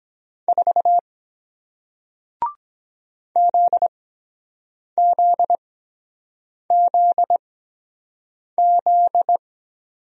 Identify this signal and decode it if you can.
Morse code, heard as a keyed tone, about 23 wpm, spaced as for 7 wpm, 700 Hz. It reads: 4 ZZZZ